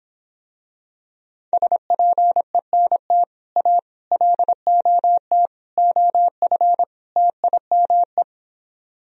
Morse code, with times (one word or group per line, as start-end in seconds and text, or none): 1.5-3.2 SPENT
3.6-3.8 A
4.1-5.5 LOT
5.8-6.8 OF
7.2-8.2 TIME